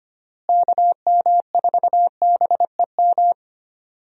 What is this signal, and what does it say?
Morse code, heard as a keyed tone, about 25 wpm, 700 Hz. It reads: KM4BEM
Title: USA Call Signs - 6 Characters - Repeated 2x - 25wpm